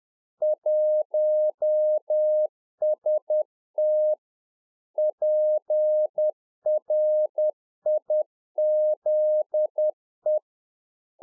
Morse code, heard as a keyed tone, about 10 wpm, 610 Hz. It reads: 1ST PRIZE